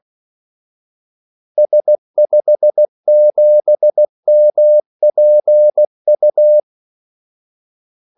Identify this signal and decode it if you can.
Morse code, heard as a keyed tone, about 16 wpm, 600 Hz. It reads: S57MPU
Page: Kaldesignaler 16 wpm